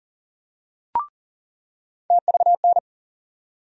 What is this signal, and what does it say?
Morse code, heard as a keyed tone, about 40 wpm, 700 Hz. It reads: TVN
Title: USA Call Sign Postfixes - 3 Letters 40wpm